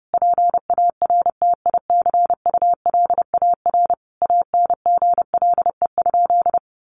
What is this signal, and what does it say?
Morse code, heard as a keyed tone, about 30 wpm, 700 Hz. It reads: PARTICULAR ANGLE?
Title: Sentences from Top 1500 Words 30wpm